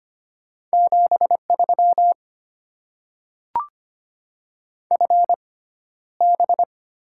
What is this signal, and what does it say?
Morse code, heard as a keyed tone, about 25 wpm, spaced as for 13 wpm, 700 Hz. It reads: 73 FB
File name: Top 20 QSO Elements - 6x Character Spacing - 25wpm